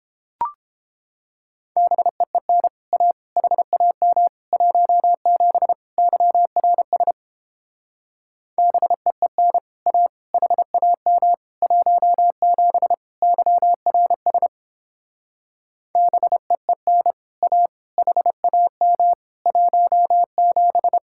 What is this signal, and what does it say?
Morse code, heard as a keyed tone, about 31 wpm, 700 Hz. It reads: BEEN A HAM 17 YRS BEEN A HAM 17 YRS BEEN A HAM 17